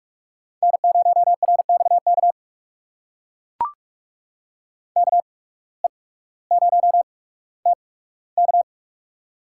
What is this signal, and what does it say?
Morse code, heard as a keyed tone, about 44 wpm, spaced as for 19 wpm, 700 Hz. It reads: N0RXK KE0TK
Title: USA Call Signs - 8x Character Spacing - 45wpm